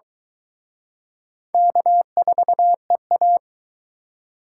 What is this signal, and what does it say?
Morse code, heard as a keyed tone, about 23 wpm, 700 Hz. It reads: K4EA